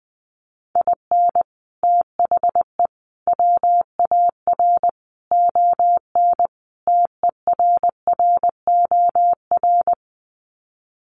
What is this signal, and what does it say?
Morse code, heard as a keyed tone, about 20 wpm, 700 Hz. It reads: IN THE WAR ON TERROR
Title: Sets of 5 Words Encore2 20wpm